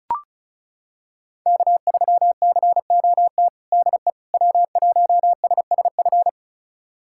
Morse code, heard as a keyed tone, about 36 wpm, 700 Hz. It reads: K3COT DE W1SSF
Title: USA Callsign DE Callsign 35wpm